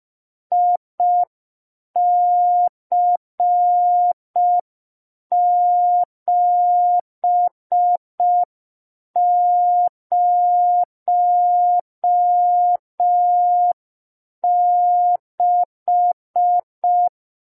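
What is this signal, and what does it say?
Morse code, heard as a keyed tone, about 5 wpm, 710 Hz. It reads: IC706
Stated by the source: Morse receiving practice lesson for an amateur radio Foundation assessment.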